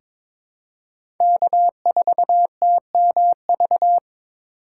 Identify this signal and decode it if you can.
Morse code, heard as a keyed tone, about 22 wpm, 700 Hz. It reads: K4TMV
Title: USA Call Signs - 5 Characters - 3x Character Spacing - 22wpm